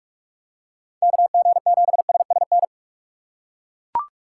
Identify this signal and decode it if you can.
Morse code, heard as a keyed tone, about 41 wpm, 700 Hz. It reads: KG7SSN
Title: USA Call Signs - 6 Characters - 3x Character Spacing - 45wpm